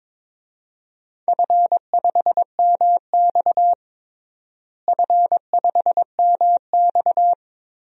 Morse code, heard as a keyed tone, about 22 wpm, 700 Hz. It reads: F5MX F5MX
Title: DXCC-100 Call Signs - 4 Characters - Repeated 3x - 22wpm